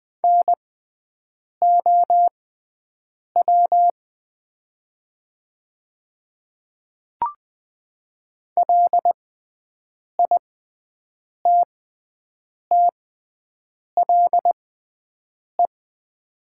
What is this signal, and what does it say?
Morse code, heard as a keyed tone, about 21 wpm, spaced as for 7 wpm, 700 Hz. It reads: NOW LITTLE
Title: 6x Character Spacing - Top 200 Words - 20wpm